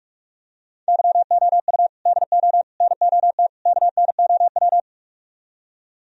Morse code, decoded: YOU DO NOT KNOW

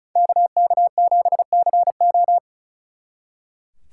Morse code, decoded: KK7CO

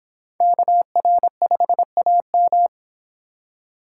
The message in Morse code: KR5AM

